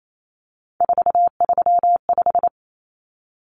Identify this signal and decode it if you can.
Morse code, heard as a keyed tone, about 28 wpm, 700 Hz. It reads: 435
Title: Common Cut Numbers - 3-Digit 28wpm